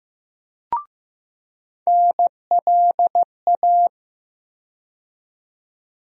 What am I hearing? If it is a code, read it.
Morse code, heard as a keyed tone, about 15 wpm, 700 Hz. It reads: NLA
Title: USA Call Sign Postfixes - 3 Letters 15wpm